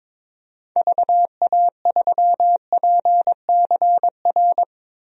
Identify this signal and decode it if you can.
Morse code, heard as a keyed tone, about 22 wpm, 700 Hz. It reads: VA3PCR
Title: DXCC-100 Call Signs - 6 Characters - Repeated 3x - 22wpm